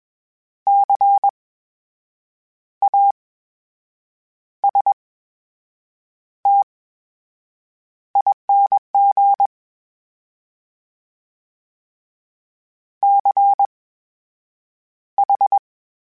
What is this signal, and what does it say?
Morse code, heard as a keyed tone, about 21 wpm, spaced as for 7 wpm, 800 Hz. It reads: CASTING CH
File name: Special INGs-mp3s / Special INGs-mp3s / VerySlowToFaster Ing @20